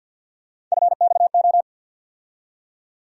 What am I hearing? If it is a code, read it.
Morse code, heard as a keyed tone, about 53 wpm, 700 Hz. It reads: FXO
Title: USA Call Sign Postfixes - 3 Letters 50wpm